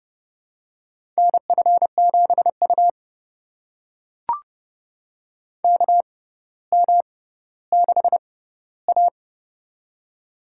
Morse code, heard as a keyed tone, about 30 wpm, spaced as for 15 wpm, 700 Hz. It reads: NF7U KM6A
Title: USA Call Signs - 4 Characters - 6x Character Spacing - 30wpm